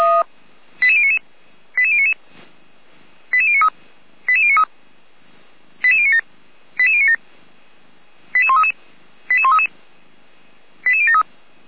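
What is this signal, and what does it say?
Five-tone selective call (ZVEI), heard as quick runs of five tones, tones of about 70 ms. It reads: 88099 88099 88093 88093 88097 88097 88133 88133 88073
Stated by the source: ZVEI five-tone selective call received via SDR, from the Feuerwehr Orlamünde audio server.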